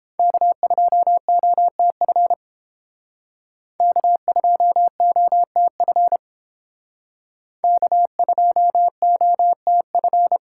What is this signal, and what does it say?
Morse code, heard as a keyed tone, about 28 wpm, 700 Hz. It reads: K2OTF K2OTF K2OTF